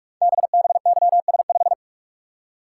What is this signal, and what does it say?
Morse code, heard as a keyed tone, about 44 wpm, 700 Hz. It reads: BBYS5
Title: Code Groups - 5 Letter:Numbers - 45wpm